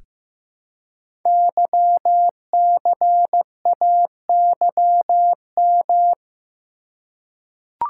Morse code, heard as a keyed tone, about 15 wpm, 700 Hz. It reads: YCAYM
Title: Code Groups - 5 Letters - 15wpm